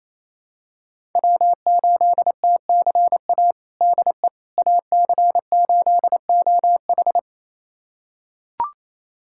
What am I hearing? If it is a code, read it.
Morse code, heard as a keyed tone, about 28 wpm, 700 Hz. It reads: W8TCA DE AC8OH
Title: USA Callsign DE Callsign - 5 Characters - 28wpm